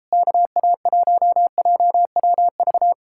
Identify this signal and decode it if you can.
Morse code, heard as a keyed tone, about 38 wpm, 700 Hz. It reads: KA1JWV